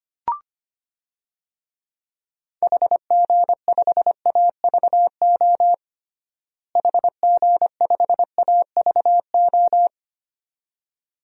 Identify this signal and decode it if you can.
Morse code, heard as a keyed tone, about 25 wpm, 700 Hz. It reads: HG5AVO HG5AVO
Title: DXCC-100 Call Signs - 6 Characters - Repeated 2x - 25wpm